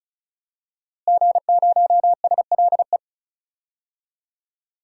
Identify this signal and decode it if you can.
Morse code, heard as a keyed tone, about 34 wpm, 700 Hz. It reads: G0SLE
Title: DXCC-100 Call Signs - 5 Characters - Repeated 2x - 35wpm